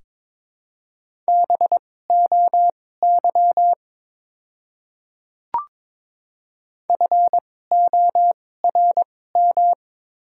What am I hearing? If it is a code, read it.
Morse code, heard as a keyed tone, about 22 wpm, spaced as for 16 wpm, 700 Hz. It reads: BOY FORM
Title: 2x Character Spacing - Top 500 Words - 22wpm